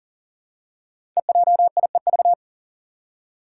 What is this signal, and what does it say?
Morse code, heard as a keyed tone, about 40 wpm, 700 Hz. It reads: EJIEV
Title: Five Challenging Letter-Number Combinations 40wpm